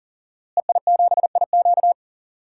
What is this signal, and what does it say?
Morse code, heard as a keyed tone, about 38 wpm, 700 Hz. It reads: EI7IQ
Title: DXCC-100 Call Signs - 5 Characters - Repeated 3x - 40wpm